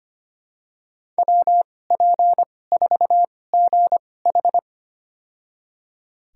Morse code, heard as a keyed tone, about 25 wpm, spaced as for 18 wpm, 700 Hz. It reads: WP4GH
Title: USA Call Signs - 2x Character Spacing - 25wpm